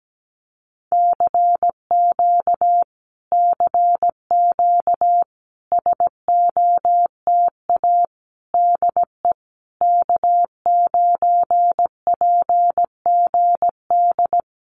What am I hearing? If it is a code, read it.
Morse code, heard as a keyed tone, about 17 wpm, 700 Hz. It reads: CQ CQ SOTA DE K9PGD
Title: Common QSO Phrases Encore 17wpm